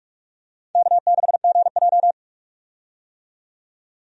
Morse code, heard as a keyed tone, about 44 wpm, 700 Hz. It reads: K6GJ
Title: USA Call Signs - 4 Characters - 45wpm